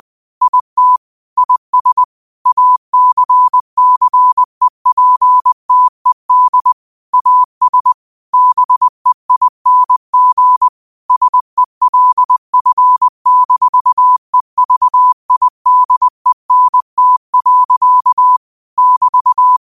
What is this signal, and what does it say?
Morse code, heard as a keyed tone, about 20 wpm, 1000 Hz. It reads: IT IS ACCEPTED AS BEING SELF-EVIDENT. <BT>
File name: qotdfiles / 20wpm